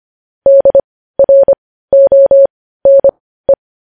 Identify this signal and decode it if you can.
Morse code, heard as a keyed tone, about 25 wpm, spaced as for 15 wpm, 550 Hz. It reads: DRONE
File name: podcast / News headlines-1x-25x15